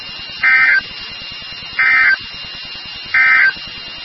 Emergency Alert System (SAME) tones, heard as harsh buzzing bursts. It is heard from 0.3 to 3.6 s.
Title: NOAA Weather Radio WXK36